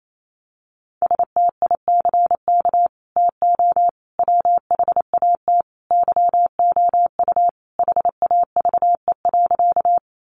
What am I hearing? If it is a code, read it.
Morse code, heard as a keyed tone, about 28 wpm, 700 Hz. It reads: STICK TO WHAT YOU HAVE.
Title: Sentences from Top 1200 Words 28wpm